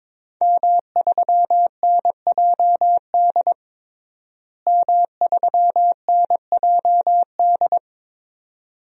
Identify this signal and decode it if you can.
Morse code, heard as a keyed tone, about 21 wpm, 700 Hz. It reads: M3NJD M3NJD
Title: DXCC-100 Call Signs - 5 Characters - Repeated 2x - 22wpm